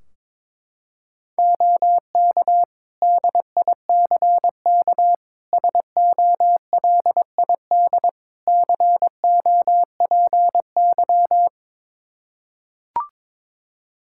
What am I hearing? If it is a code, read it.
Morse code, heard as a keyed tone, about 22 wpm, 700 Hz. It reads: OK DICK SOLID COPY
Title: Practice QSO - Exchange 3 - More Info - 22wpm